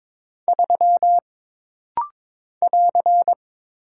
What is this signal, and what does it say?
Morse code, heard as a keyed tone, about 22 wpm, 700 Hz. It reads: 3 <AR>
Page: This rapid-fire timing is geared to push your ICR (Instant Character Recognition) skills.